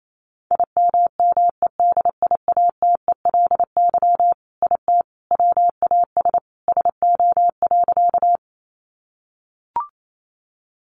Morse code, heard as a keyed tone, about 28 wpm, 700 Hz. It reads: IMMEDIATELY IT WAS SO.
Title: Sentences from Top 1100 Words 28wpm